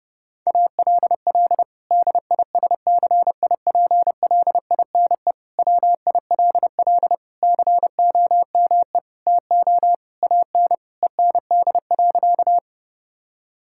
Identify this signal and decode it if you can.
Morse code, heard as a keyed tone, about 30 wpm, 700 Hz. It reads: ALL DISCIPLINE WILL COME TO AN END.
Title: Sentences from Top 1000 Words - Review Part 6 - 30wpm